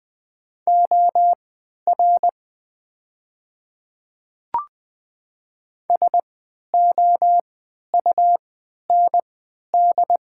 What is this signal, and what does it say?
Morse code, heard as a keyed tone, about 20 wpm, spaced as for 11 wpm, 700 Hz. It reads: OR SOUND
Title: 3x Character Spacing - Top 200 Words - 20wpm